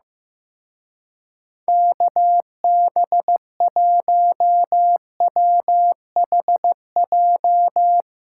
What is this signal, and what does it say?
Morse code, heard as a keyed tone, about 16 wpm, 700 Hz. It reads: KB1WHJ
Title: Speed Racing - USA 2x3 Call Signs 15wpm